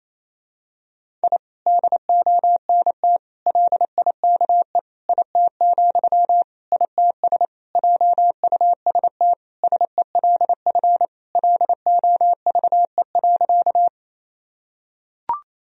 Morse code, heard as a keyed tone, about 28 wpm, 700 Hz. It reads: I DONT LIKE IT, ITS JUST SELF LOVE.
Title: Sentences from Top 1000 Words - Review Part 1 - 28wpm